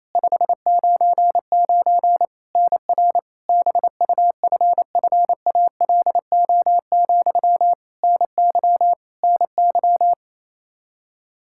Morse code, heard as a keyed tone, about 27 wpm, 700 Hz. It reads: 599 NR BUFFALO, NY NY
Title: K1USN SST QSO Practice 28wpm